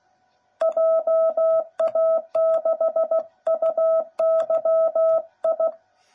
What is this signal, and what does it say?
Morse code, heard as a keyed tone, about 16 wpm, 660 Hz. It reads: JA6UYI
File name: jh8cbh / 2021cw / cw09b